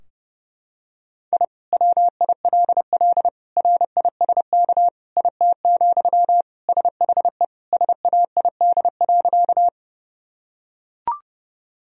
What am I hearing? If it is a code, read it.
Morse code, heard as a keyed tone, about 30 wpm, 700 Hz. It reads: I WILL RISK IT, SHE SAID.
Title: Sentences from Top 1000 Words - Review Part 4 - 30wpm